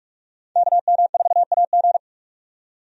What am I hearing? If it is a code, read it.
Morse code, heard as a keyed tone, about 42 wpm, 700 Hz. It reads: KM4AG
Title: USA Call Signs - 5 Characters - 45wpm